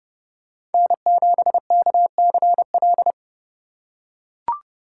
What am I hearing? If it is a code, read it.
Morse code, heard as a keyed tone, about 29 wpm, 700 Hz. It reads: N7KCL